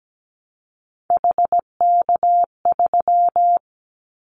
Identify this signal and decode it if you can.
Morse code, heard as a keyed tone, about 17 wpm, 700 Hz. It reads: HK3
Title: Callsign Prefixes for Top 100 DXCC - 17wpm